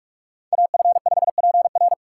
Morse code, heard as a keyed tone, about 46 wpm, 700 Hz. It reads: AF5PR
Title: USA Call Signs - 5 Characters - 3x Character Spacing - 45wpm